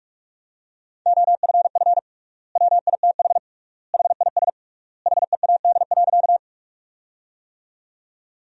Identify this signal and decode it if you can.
Morse code, heard as a keyed tone, about 46 wpm, 700 Hz. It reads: OFF WITH HIS HEAD.
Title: 3x Word-Spacing - Sentences from Top 300 Words - 45wpm